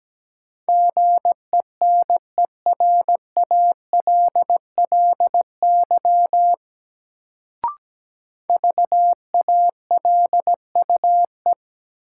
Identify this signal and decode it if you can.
Morse code, heard as a keyed tone, about 17 wpm, 700 Hz. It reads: GENERALLY VALUE
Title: Sentences from Top 600 Words - New Word Drill - 17wpm